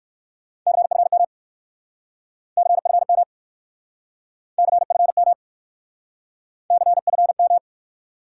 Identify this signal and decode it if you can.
Morse code, heard as a keyed tone, about 60 wpm, 700 Hz. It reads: CFM CFM CFM CFM